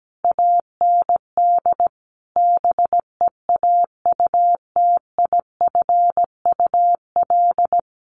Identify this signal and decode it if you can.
Morse code, heard as a keyed tone, about 17 wpm, 700 Hz. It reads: AND BEAUTIFUL
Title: Sentences from Top 1100 Words 17wpm